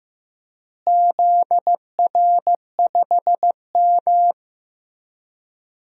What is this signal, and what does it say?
Morse code, heard as a keyed tone, about 15 wpm, 700 Hz. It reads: ZR5M
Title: DXCC-100 Call Signs - 4 Characters - Repeated 3x - 15wpm